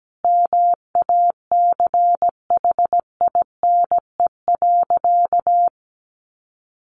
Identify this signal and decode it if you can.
Morse code, heard as a keyed tone, about 17 wpm, 700 Hz. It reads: MACHINE.
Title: A Sound Of Thunder 17wpm